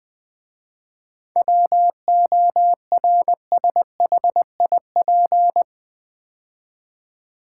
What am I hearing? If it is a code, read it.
Morse code, heard as a keyed tone, about 20 wpm, 700 Hz. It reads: WORSHIP
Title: 7 Letter Words - First 3 Letters Given - 20wpm